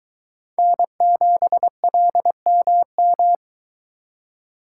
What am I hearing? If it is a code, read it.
Morse code, heard as a keyed tone, about 22 wpm, 700 Hz. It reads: N7LMM